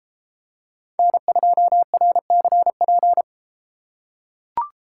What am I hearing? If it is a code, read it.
Morse code, heard as a keyed tone, about 33 wpm, 700 Hz. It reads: N2RCP